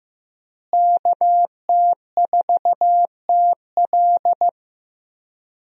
Morse code, heard as a keyed tone, about 15 wpm, 700 Hz. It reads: KT4TL
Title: USA Call Signs - 5 Characters - Repeated 2x - 15wpm